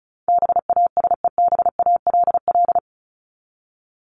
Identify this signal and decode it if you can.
Morse code, heard as a keyed tone, about 34 wpm, 700 Hz. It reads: BASEBALL
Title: Top 1751-2000 Words 35wpm